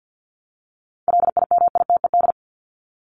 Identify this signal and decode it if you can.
Morse code, heard as a keyed tone, about 50 wpm, 700 Hz. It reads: LIMITED
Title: ED Suffix - Top 100 Words with Base Words 50wpm